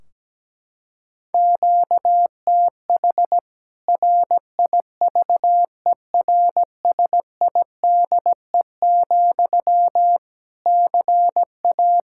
Morse code, heard as a keyed tone, about 17 wpm, 700 Hz. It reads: QTH RIVERSIDE, CA
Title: K1USN SST QSO Practice 17wpm